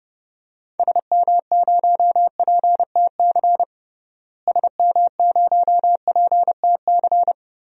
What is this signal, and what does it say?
Morse code, heard as a keyed tone, about 32 wpm, 700 Hz. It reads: SM0PTC SM0PTC
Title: DXCC-100 Call Signs - 6 Characters - Repeated 3x - 30wpm